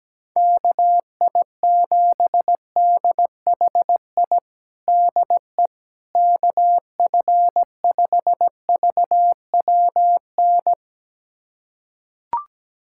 Morse code, heard as a keyed tone, about 17 wpm, 700 Hz. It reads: KI7DHI DE KF5VWN
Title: USA Callsign DE Callsign - 6 Characters - 17wpm